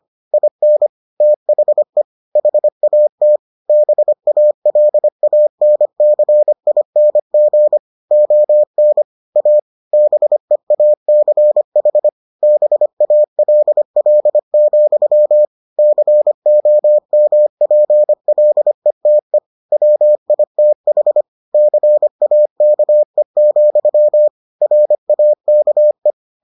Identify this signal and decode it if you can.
Morse code, heard as a keyed tone, about 25 wpm, 600 Hz. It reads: IN THE HAT BALANCING ON A BEACH BALL, COMPLETE WITH CAKE, RAKE